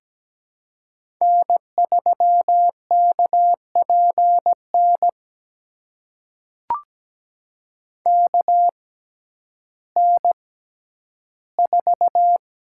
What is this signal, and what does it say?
Morse code, heard as a keyed tone, about 17 wpm, spaced as for 10 wpm, 700 Hz. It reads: N3KPN KN4
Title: USA Call Signs - 5 Characters - 6x Character Spacing - 17wpm